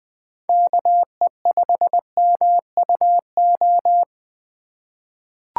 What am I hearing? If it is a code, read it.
Morse code, heard as a keyed tone, about 20 wpm, 700 Hz. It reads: KE5MUO
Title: USA Call Signs - 6 Characters - 2x Character Spacing - 20wpm